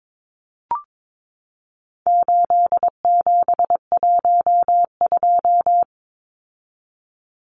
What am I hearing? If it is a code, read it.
Morse code, heard as a keyed tone, about 22 wpm, 700 Hz. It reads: 8712
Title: Numbers - 4-Digit 22wpm